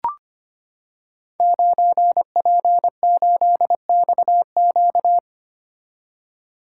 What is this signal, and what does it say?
Morse code, heard as a keyed tone, about 25 wpm, 700 Hz. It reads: 9P8XQ
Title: Code Groups - 5 Letter:Numbers - 25wpm